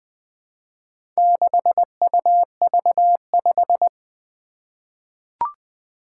Morse code, heard as a keyed tone, about 20 wpm, 700 Hz. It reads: 6UV5
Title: Four Challenging Letter-Number Combinations 20wpm